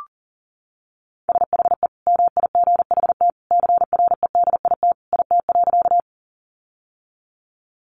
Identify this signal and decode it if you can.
Morse code, heard as a keyed tone, about 40 wpm, 700 Hz. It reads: SHE MIGHT CREDIT IT.
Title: Sentences from Top 1000 Words 40wpm